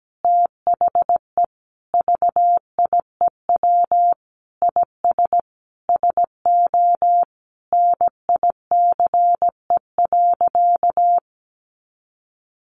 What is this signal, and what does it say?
Morse code, heard as a keyed tone, about 17 wpm, 700 Hz. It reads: THE VIEW IS SO NICE.
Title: Additional Easy Phrases 17wpm